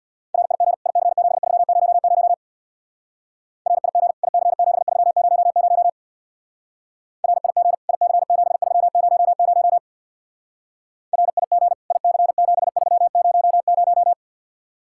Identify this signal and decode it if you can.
Morse code, heard as a keyed tone, about 58 wpm, 700 Hz. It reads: RIG IC7300 RIG IC7300 RIG IC7300 RIG IC7300
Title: Speed-Racing Common QSO Phrases - 50wpm